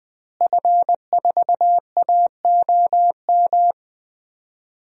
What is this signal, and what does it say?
Morse code, heard as a keyed tone, about 20 wpm, 700 Hz. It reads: F4AOM
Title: DXCC-100 Call Signs - 5 Characters - Repeated 2x - 20wpm